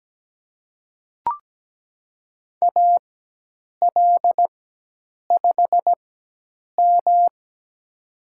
Morse code, heard as a keyed tone, about 17 wpm, spaced as for 8 wpm, 700 Hz. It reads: AL5M
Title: USA Call Signs - 4 Characters - 4x Character Spacing - 17wpm